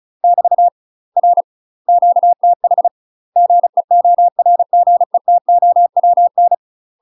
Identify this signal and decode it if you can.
Morse code, heard as a keyed tone, about 35 wpm, 700 Hz. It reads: <BT> R QTH GEORGETOWN